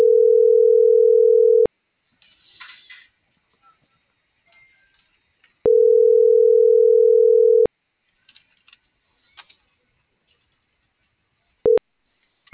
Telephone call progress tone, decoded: ringback tone

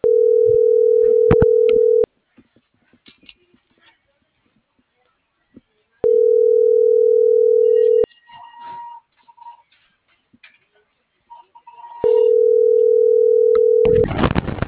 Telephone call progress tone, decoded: ringback tone

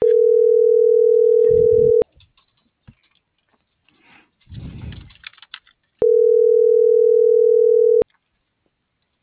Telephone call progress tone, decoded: ringback tone

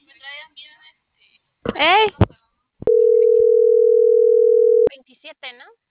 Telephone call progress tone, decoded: ringback tone

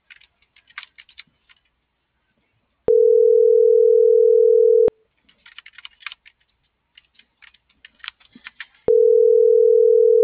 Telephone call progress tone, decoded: ringback tone

